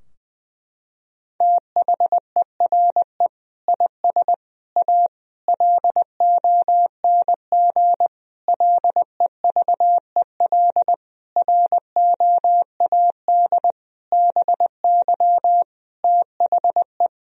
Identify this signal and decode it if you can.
Morse code, heard as a keyed tone, about 21 wpm, 700 Hz. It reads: THERE IS A LONG LEVEL ROAD BY THE
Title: Sentences from Top 1000 Words - Review Part 4 - 20wpm